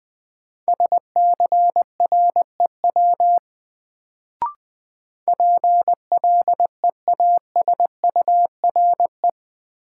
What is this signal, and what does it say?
Morse code, heard as a keyed tone, about 21 wpm, 700 Hz. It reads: SCREW PLEASURE
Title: Sentences from Top 1600 Words - New Word Drill Alt - 20wpm